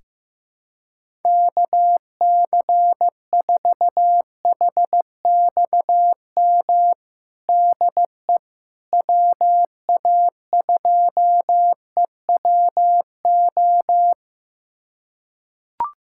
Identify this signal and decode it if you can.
Morse code, heard as a keyed tone, about 16 wpm, 700 Hz. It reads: KC4HXM DE WA2EWO
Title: USA Callsign DE Callsign 15wpm